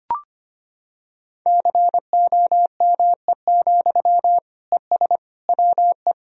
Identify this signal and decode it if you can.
Morse code, heard as a keyed tone, about 25 wpm, 700 Hz. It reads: COME, ES WE
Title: Sentences from Top 500 Words and Abbreviations 25wpm